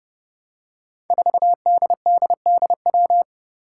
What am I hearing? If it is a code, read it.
Morse code, heard as a keyed tone, about 30 wpm, 700 Hz. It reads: 4DDDW
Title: Five Challenging Letter-Number Combinations 30wpm